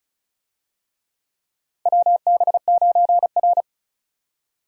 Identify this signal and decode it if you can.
Morse code, heard as a keyed tone, about 36 wpm, 700 Hz. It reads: WB9R
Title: USA Call Signs - 4 Characters - 35wpm